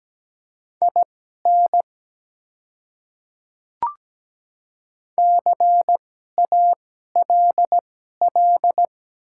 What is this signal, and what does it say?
Morse code, heard as a keyed tone, about 17 wpm, spaced as for 12 wpm, 700 Hz. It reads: IN CALL
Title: 2x Character Spacing - Top 100 Words - 17wpm